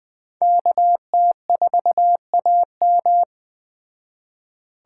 Morse code, heard as a keyed tone, about 20 wpm, 700 Hz. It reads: KT4AM
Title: Speed Racing - USA 2x2 Call Signs 17wpm